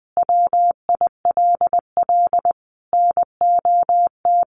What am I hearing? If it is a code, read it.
Morse code, heard as a keyed tone, about 20 wpm, 700 Hz. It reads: WILL NOT
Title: Sentences from Top 600 Words 20wpm